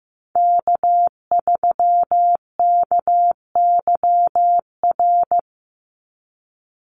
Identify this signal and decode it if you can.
Morse code, heard as a keyed tone, about 15 wpm, 700 Hz. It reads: K3KYR
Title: NA SOTA Chaser Callsigns 15wpm